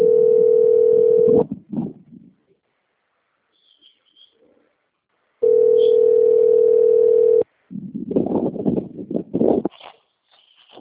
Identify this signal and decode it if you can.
Ringback tone (North America), 440 + 480 Hz.